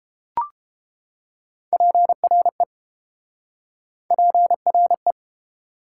Morse code, heard as a keyed tone, about 31 wpm, 690 Hz. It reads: PRE PRE